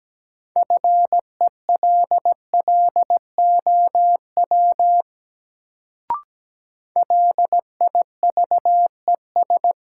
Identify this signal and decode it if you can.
Morse code, heard as a keyed tone, about 17 wpm, 700 Hz. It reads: FELLOW LIVES
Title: Sets of 5 Words Encore - New Word Drill - 17wpm